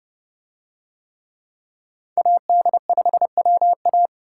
Morse code, heard as a keyed tone, about 28 wpm, 700 Hz. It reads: AD5WA